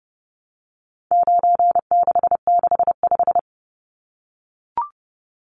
Morse code, heard as a keyed tone, about 30 wpm, 700 Hz. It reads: 9665
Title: Numbers - 4-Digit 30wpm